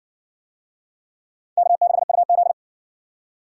Morse code, heard as a keyed tone, about 60 wpm, 700 Hz. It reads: K6UZ